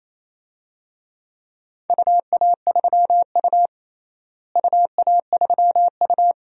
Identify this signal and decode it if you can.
Morse code, heard as a keyed tone, about 27 wpm, 700 Hz. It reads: UA3U UA3U
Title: DXCC-100 Call Signs - 4 Characters - Repeated 2x - 28wpm